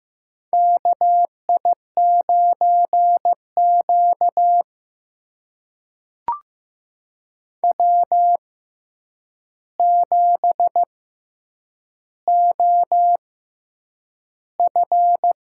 Morse code, heard as a keyed tone, about 15 wpm, spaced as for 8 wpm, 700 Hz. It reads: KI9Q W7OF
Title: USA Call Signs - 4 Characters - 6x Character Spacing - 15wpm